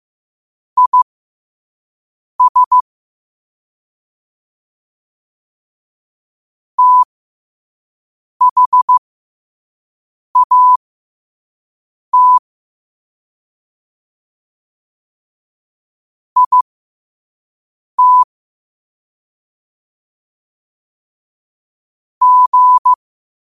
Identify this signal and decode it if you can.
Morse code, heard as a keyed tone, about 15 wpm, spaced as for 5 wpm, 1000 Hz. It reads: IS THAT IT G
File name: qotdfiles / 5wpm